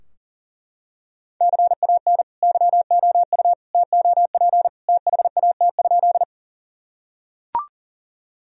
Morse code, heard as a keyed tone, about 40 wpm, 700 Hz. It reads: CAN YOU TOP THAT?
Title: Sentences from Top 1000 Words - Review Part 2 - 40wpm